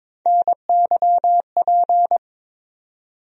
Morse code, heard as a keyed tone, about 22 wpm, 700 Hz. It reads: NYP
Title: USA Call Sign Postfixes - 3 Letters 22wpm